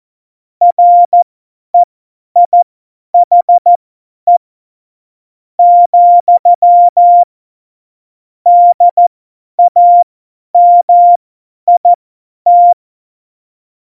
Morse code, heard as a keyed tone, about 14 wpm, spaced as for 10 wpm, 700 Hz. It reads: REIHE , DAMIT
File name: Morsekurs / Witze / Witz 28. Mai 2019, 50 F70